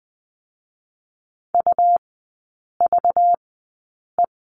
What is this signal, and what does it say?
Morse code, heard as a keyed tone, about 20 wpm, spaced as for 8 wpm, 700 Hz. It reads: UVE
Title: 218 Lesson 35e - Number 0 - Three Characters 20wpm